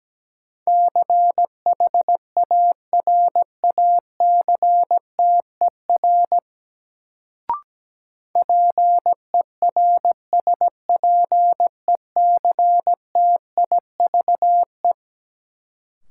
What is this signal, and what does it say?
Morse code, heard as a keyed tone, about 17 wpm, 700 Hz. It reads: CHARACTER PERSPECTIVE
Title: Sentences from Top 1200 Words - New Word Drill - 17wpm